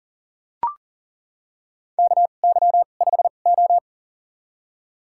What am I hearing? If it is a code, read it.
Morse code, heard as a keyed tone, about 40 wpm, spaced as for 29 wpm, 690 Hz. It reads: KY5O
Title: USA Call Signs - 4 Characters - 2x Character Spacing - 40wpm